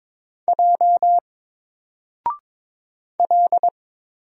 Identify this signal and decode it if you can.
Morse code, heard as a keyed tone, about 22 wpm, 700 Hz. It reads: J L
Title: Single Character (Letter:Number:Punctuation:Prosign) 22wpm